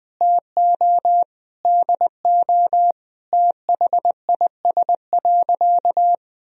Morse code, heard as a keyed tone, about 21 wpm, 700 Hz. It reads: TO DO THIS.
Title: Sentences from Top 1000 Words - Review Part 2 - 20wpm